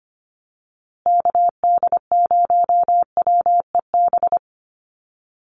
Morse code, heard as a keyed tone, about 25 wpm, 700 Hz. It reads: KD0WEB
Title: USA Call Signs Repeated 2x 25wpm